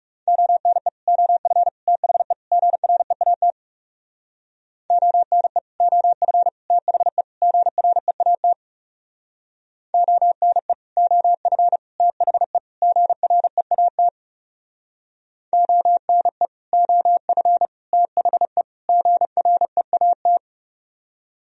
Morse code, heard as a keyed tone, about 37 wpm, 700 Hz. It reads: ONE OF THE GREAT ONE OF THE GREAT ONE OF THE GREAT ONE OF THE GREAT
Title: Speed-Racing - Sets of 4 Words 30wpm